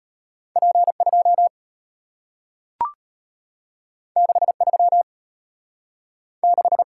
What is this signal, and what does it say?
Morse code, heard as a keyed tone, about 34 wpm, 700 Hz. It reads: P2 63 6